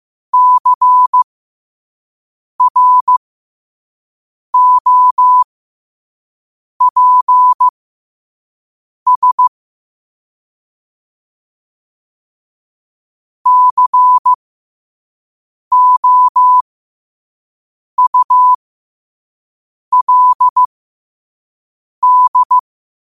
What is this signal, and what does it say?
Morse code, heard as a keyed tone, about 15 wpm, spaced as for 5 wpm, 1000 Hz. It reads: CROPS COULD
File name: qotdfiles / 5wpm